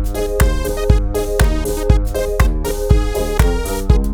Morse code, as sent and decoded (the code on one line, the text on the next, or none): none